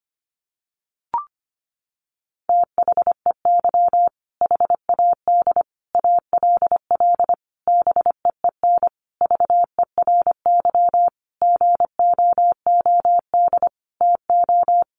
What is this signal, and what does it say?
Morse code, heard as a keyed tone, about 25 wpm, 700 Hz. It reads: THEY HAD ALL BEEN VERY GOOD TO
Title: Sentences from Top 200 Words 25wpm